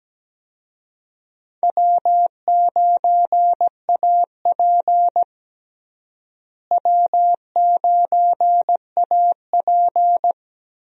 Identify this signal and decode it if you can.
Morse code, heard as a keyed tone, about 17 wpm, 700 Hz. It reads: W9AP W9AP
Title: USA Call Signs - 4 Characters - Repeated 3x - 17wpm